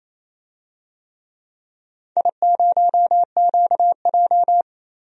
Morse code, heard as a keyed tone, about 28 wpm, 700 Hz. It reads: I0QJ
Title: DXCC-100 Call Signs - 4 Characters - 28wpm